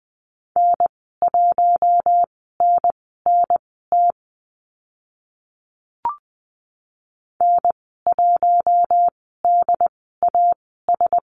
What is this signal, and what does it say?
Morse code, heard as a keyed tone, about 20 wpm, spaced as for 14 wpm, 700 Hz. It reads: N1NNT N1DAS
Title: 076 Lesson 13g - Letter L - Call Signs 20wpm